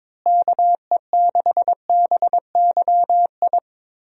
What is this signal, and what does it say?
Morse code, heard as a keyed tone, about 22 wpm, 700 Hz. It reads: KE6BYI
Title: USA Call Signs - 6 Characters - Repeated 3x - 22wpm